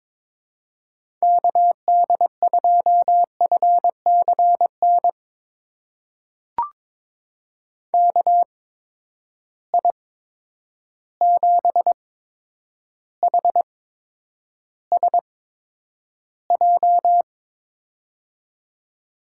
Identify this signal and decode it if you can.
Morse code, heard as a keyed tone, about 22 wpm, spaced as for 9 wpm, 700 Hz. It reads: KD2FCN KI7HSJ
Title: USA Call Signs - 6 Characters - 8x Character Spacing - 22wpm